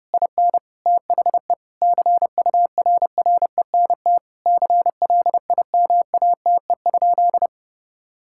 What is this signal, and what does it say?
Morse code, heard as a keyed tone, about 30 wpm, 700 Hz. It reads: IN THE CURRENT CLIMATE?
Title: Sentences from Top 1600 Words 30wpm